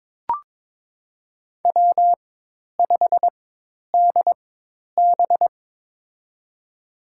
Morse code, heard as a keyed tone, about 22 wpm, spaced as for 10 wpm, 700 Hz. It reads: W5DB
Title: USA Call Signs - 4 Characters - 4x Character Spacing - 22wpm